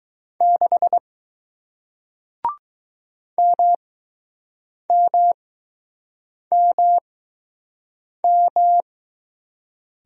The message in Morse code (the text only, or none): 6 MMMM